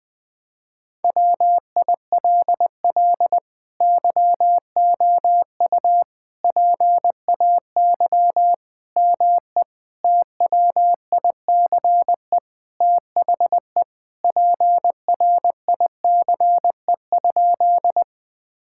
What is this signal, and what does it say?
Morse code, heard as a keyed tone, about 20 wpm, 700 Hz. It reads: WILL YOU PAY ME TWICE THE PRICE?
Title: Sentences from Top 1600 Words 20wpm